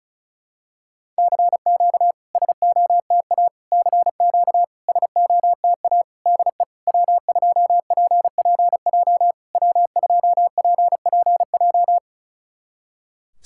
Morse code, heard as a keyed tone, about 36 wpm, 700 Hz. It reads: CQ SOTA CQ SOTA DE W2PPJ W2PPJ
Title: SOTA QSO Practice 35wpm